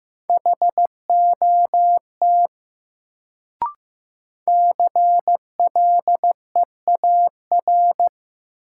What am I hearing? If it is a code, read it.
Morse code, heard as a keyed tone, about 15 wpm, 700 Hz. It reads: HOT CLEAR
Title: Sentences from Top 500 Words - New Word Drill - 15wpm